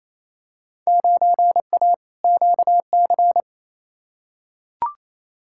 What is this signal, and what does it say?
Morse code, heard as a keyed tone, about 28 wpm, 700 Hz. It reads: 9A QC
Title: Field Day Exchanges Encore - 28wpm